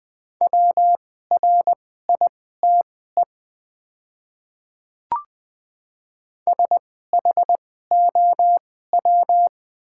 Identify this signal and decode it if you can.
Morse code, heard as a keyed tone, about 20 wpm, spaced as for 14 wpm, 700 Hz. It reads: WRITE SHOW
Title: 2x Character Spacing - Top 200 Words - 20wpm